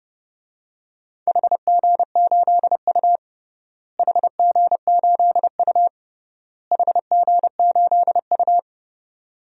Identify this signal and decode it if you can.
Morse code, heard as a keyed tone, about 28 wpm, 700 Hz. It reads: HG8U HG8U HG8U
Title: DXCC-100 Call Signs - 4 Characters - Repeated 3x - 30wpm